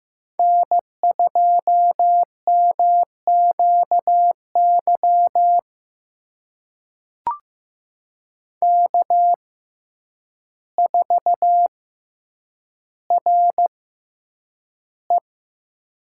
Morse code, heard as a keyed tone, about 15 wpm, spaced as for 8 wpm, 700 Hz. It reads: N2MQY K4RE
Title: USA Call Signs - 6x Character Spacing - 15wpm